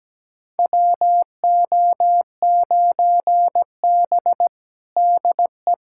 Morse code, heard as a keyed tone, about 17 wpm, 700 Hz. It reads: WO9B DE
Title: USA Callsign DE Callsign - 4 Characters - 17wpm